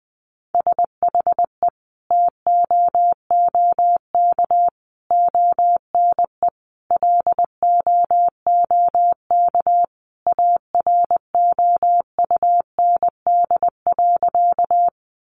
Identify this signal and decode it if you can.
Morse code, heard as a keyed tone, about 20 wpm, 700 Hz. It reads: SHE TOOK ONE LOOK AROUND.
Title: Sentences from Top 600 Words 20wpm